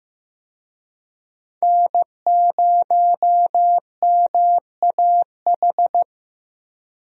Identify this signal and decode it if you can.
Morse code, heard as a keyed tone, about 15 wpm, 700 Hz. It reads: N0MAH
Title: USA Call Signs - 5 Characters - Repeated 3x - 15wpm